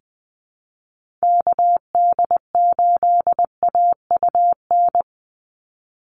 Morse code, heard as a keyed tone, about 20 wpm, 700 Hz. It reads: KD8AUN
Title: USA Call Signs 20wpm